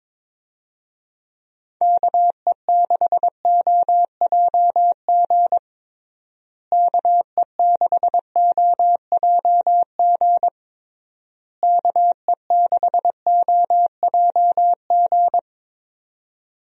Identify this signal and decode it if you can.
Morse code, heard as a keyed tone, about 22 wpm, 700 Hz. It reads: KE6OJG KE6OJG KE6OJG
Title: USA Call Signs - 6 Characters - Repeated 3x - 22wpm